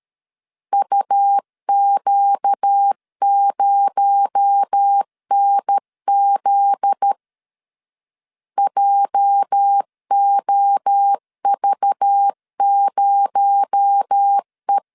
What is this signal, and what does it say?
Morse code, heard as a keyed tone, about 13 wpm, 800 Hz. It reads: UQ0NZ JOV0E
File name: LEZIONI / 8 YJ45 / 8 YJ4 12wpm 2151